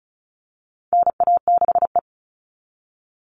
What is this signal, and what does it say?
Morse code, heard as a keyed tone, about 34 wpm, 700 Hz. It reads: NA6E
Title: All Cut Numbers - 4-Digit 35wpm